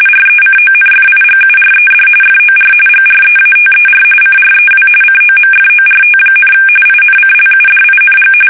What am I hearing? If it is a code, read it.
RTTY, about 45.45 baud. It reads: RECONFIGURED FOR GREATER CREW COMFORT, INCLUDING
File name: fsk850 / battleshipnj